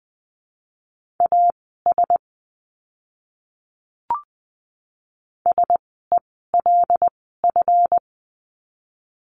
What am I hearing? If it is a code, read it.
Morse code, heard as a keyed tone, about 20 wpm, spaced as for 14 wpm, 700 Hz. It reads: AS SELF
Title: 139 Lesson 23f - Letter f - Words 20wpm